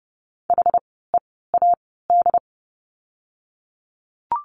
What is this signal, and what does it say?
Morse code, heard as a keyed tone, about 30 wpm, spaced as for 17 wpm, 700 Hz. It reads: HEAD
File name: course / 30:10wpm / 108 Lesson 18f - Letter m - Words 30wpm